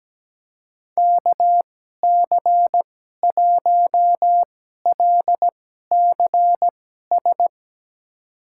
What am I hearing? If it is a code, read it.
Morse code, heard as a keyed tone, about 17 wpm, spaced as for 12 wpm, 700 Hz. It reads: KC1LCS
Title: USA Call Signs - 6 Characters - 2x Character Spacing - 17wpm